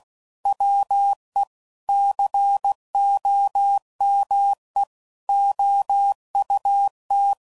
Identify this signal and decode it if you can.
Morse code, heard as a keyed tone, about 16 wpm, 800 Hz. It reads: WE COME OUT
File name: CW Code Groups @15WPM / QSOs / WD0FBT